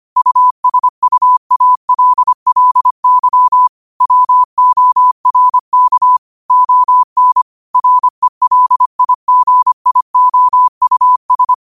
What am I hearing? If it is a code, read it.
Morse code, heard as a keyed tone, about 25 wpm, 1000 Hz. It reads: USUALLY WORK ON RELIGIOUS